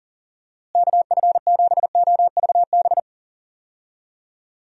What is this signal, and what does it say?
Morse code, heard as a keyed tone, about 40 wpm, 700 Hz. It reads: KF7OVB
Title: USA Call Signs - 6 Characters - Repeated 2x - 40wpm